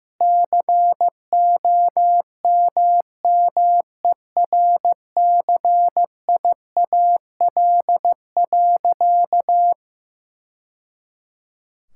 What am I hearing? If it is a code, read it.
Morse code, heard as a keyed tone, about 16 wpm, 700 Hz. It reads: COMMERCIAL.
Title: Sentences from Top 1000 Words - Review Part 5 - 15wpm